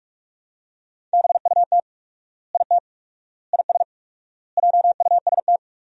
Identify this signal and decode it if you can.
Morse code, heard as a keyed tone, about 46 wpm, 700 Hz. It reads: BUT IT IS JUST
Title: 4x Word-Spacing - Sentences from Top 200 Words - 45wpm